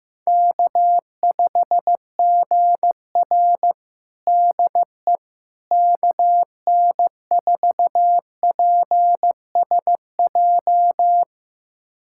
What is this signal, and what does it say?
Morse code, heard as a keyed tone, about 15 wpm, 700 Hz. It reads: K5GR DE KN4PSJ
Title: USA Callsign DE Callsign 15wpm